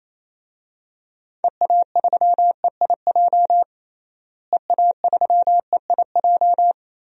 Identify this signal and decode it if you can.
Morse code, heard as a keyed tone, about 27 wpm, 700 Hz. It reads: EA3EIJ EA3EIJ
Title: DXCC-100 Call Signs - 6 Characters - Repeated 3x - 28wpm